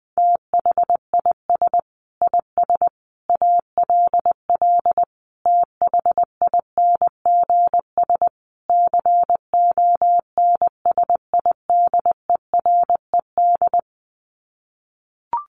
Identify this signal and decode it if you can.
Morse code, heard as a keyed tone, about 20 wpm, 700 Hz. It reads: THIS IS ALL THINGS CONSIDERED
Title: Sets of 5 Words 20wpm